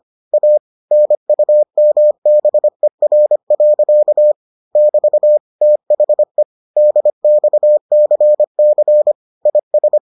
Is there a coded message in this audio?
Yes: Morse code from 0.3 to 10.0 s, heard as a keyed tone, about 25 wpm, 600 Hz. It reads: A NUMBER. <BT> THE DXCC IS